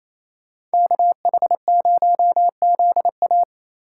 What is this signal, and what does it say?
Morse code, heard as a keyed tone, about 37 wpm, 700 Hz. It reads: KH0ZA